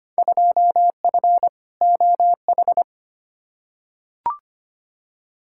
Morse code, heard as a keyed tone, about 25 wpm, 700 Hz. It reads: 2F OH
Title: Field Day Exchanges - 25wpm